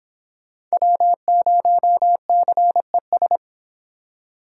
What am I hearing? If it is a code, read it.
Morse code, heard as a keyed tone, about 26 wpm, 700 Hz. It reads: W0CES